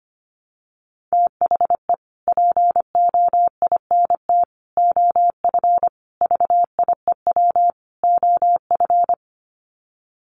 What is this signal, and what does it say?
Morse code, heard as a keyed tone, about 25 wpm, 700 Hz. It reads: THE POINT OF VIEW OF
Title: Sets of 5 Words 25wpm